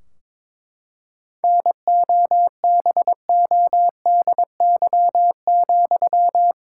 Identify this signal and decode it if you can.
Morse code, heard as a keyed tone, about 22 wpm, 700 Hz. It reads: NOBODY,
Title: Sentences from Top 1600 Words 22wpm